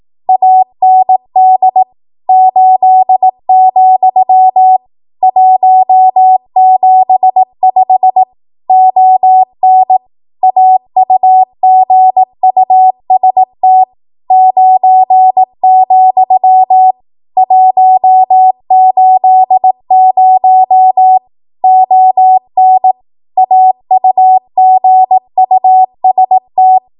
Morse code, heard as a keyed tone, about 18 wpm, 750 Hz. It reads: AND 8, 175 ON AUGUST 9, 180 ON AUGUST
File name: Archive / 18 WPM / 18WPM